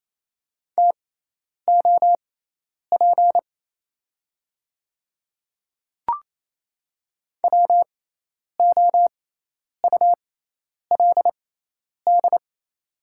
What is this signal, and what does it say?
Morse code, heard as a keyed tone, about 28 wpm, spaced as for 10 wpm, 700 Hz. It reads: TOP WOULD
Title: 6x Character Spacing - Top 500 Words - 28wpm